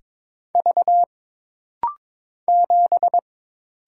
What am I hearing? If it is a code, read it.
Morse code, heard as a keyed tone, about 22 wpm, 700 Hz. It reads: V 7
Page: This rapid-fire timing is geared to push your ICR (Instant Character Recognition) skills.